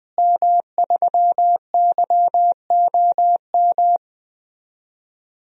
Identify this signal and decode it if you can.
Morse code, heard as a keyed tone, about 20 wpm, 700 Hz. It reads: M3YOM
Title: DXCC-100 Call Signs - 5 Characters - Repeated 3x - 20wpm